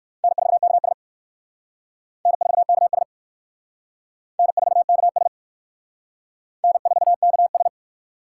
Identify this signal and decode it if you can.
Morse code, heard as a keyed tone, about 56 wpm, 700 Hz. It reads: N4KS N4KS N4KS N4KS
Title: Speed Racing - USA 1x2 Call Signs 45wpm